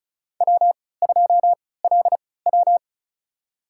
Morse code, heard as a keyed tone, about 36 wpm, spaced as for 20 wpm, 700 Hz. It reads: W2LW
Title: USA Call Signs - 4 Characters - 3x Character Spacing - 35wpm